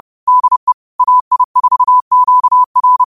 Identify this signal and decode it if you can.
Morse code, heard as a keyed tone, about 31 wpm, 1000 Hz. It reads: DE AI4QR